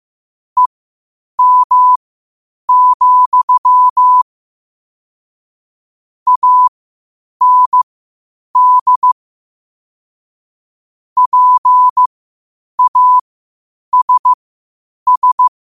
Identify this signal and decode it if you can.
Morse code, heard as a keyed tone, about 15 wpm, spaced as for 8 wpm, 1000 Hz. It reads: EM, AND PASS
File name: qotdfiles / 8wpm